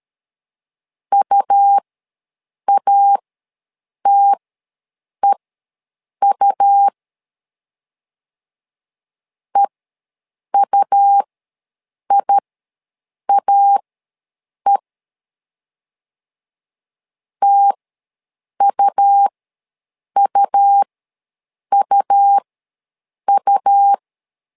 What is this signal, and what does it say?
Morse code, heard as a keyed tone, about 13 wpm, spaced as for 7 wpm, 800 Hz. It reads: UATEU EUIAE TUUUU
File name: LEZIONI / 2 UHDN / 2 U ws 12wpm 1655